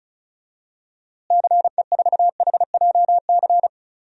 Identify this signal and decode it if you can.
Morse code, heard as a keyed tone, about 36 wpm, 700 Hz. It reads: CE4HJC
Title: DXCC-100 Call Signs - 6 Characters - Repeated 2x - 35wpm